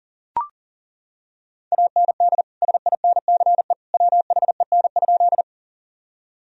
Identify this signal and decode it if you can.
Morse code, heard as a keyed tone, about 40 wpm, 700 Hz. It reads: AND SINCE WHEN?
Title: Sentences from Top 1000 Words - Review Part 1 - 40wpm